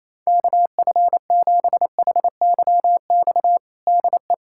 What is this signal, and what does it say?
Morse code, heard as a keyed tone, about 28 wpm, 700 Hz. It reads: KF7HYX DE
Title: USA Callsign DE Callsign - 6 Characters - 28wpm